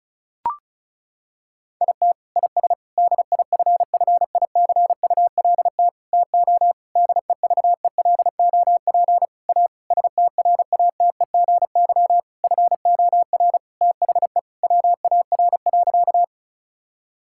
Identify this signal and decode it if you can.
Morse code, heard as a keyed tone, about 36 wpm, 700 Hz. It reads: IT IS DIFFICULT TO DEVELOP A STRATEGY FOR THE WAR.
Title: Sentences from Top 1000 Words - Review Part 5 - 35wpm